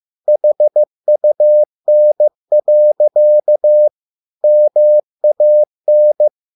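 Morse code, heard as a keyed tone, about 15 wpm, 600 Hz. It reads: HUN. MAN
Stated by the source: Morse code at a slow 15 wpm from CW med Gnister.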